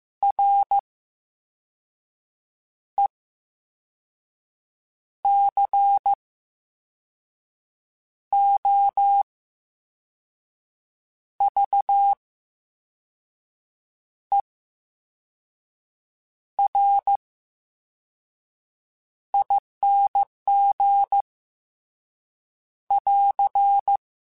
Morse code, heard as a keyed tone, about 15 wpm, spaced as for 4 wpm, 800 Hz. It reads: RECOVERING <AR>
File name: Special INGs / Special INGs-mp3s / Slow ing @13